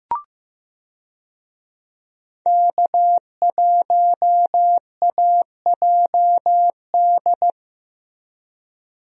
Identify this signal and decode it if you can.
Morse code, heard as a keyed tone, about 15 wpm, 700 Hz. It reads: K1AJD
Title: USA Call Signs - 5 Characters - Repeated 2x - 15wpm